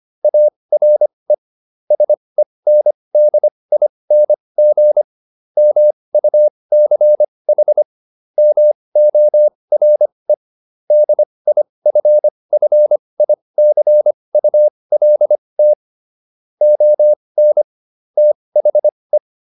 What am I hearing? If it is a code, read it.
Morse code, heard as a keyed tone, about 25 wpm, spaced as for 20 wpm, 600 Hz. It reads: ARE SENDING MUCH MORE DIFFICULT ON THE